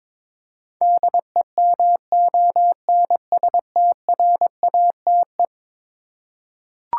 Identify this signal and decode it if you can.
Morse code, heard as a keyed tone, about 22 wpm, 700 Hz. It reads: DEMONSTRATE
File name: practice / 22wpm / Eleven Letter Words - 22wpm